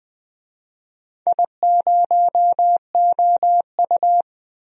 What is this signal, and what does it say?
Morse code, heard as a keyed tone, about 20 wpm, 700 Hz. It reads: I0OU
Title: DXCC-100 Call Signs - 4 Characters - Repeated 2x - 20wpm